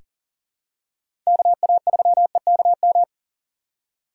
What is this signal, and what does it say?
Morse code, heard as a keyed tone, about 39 wpm, 700 Hz. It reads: KA3EKM
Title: USA Call Signs - 6 Characters - 3x Character Spacing - 40wpm